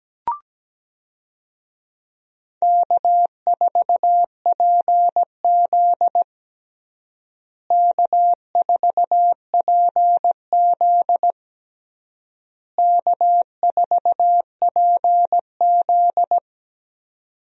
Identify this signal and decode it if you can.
Morse code, heard as a keyed tone, about 17 wpm, 700 Hz. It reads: K4PZ K4PZ K4PZ
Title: USA Call Signs - 4 Characters - Repeated 3x - 17wpm